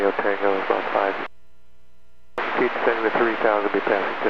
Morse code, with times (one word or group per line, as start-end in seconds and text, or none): none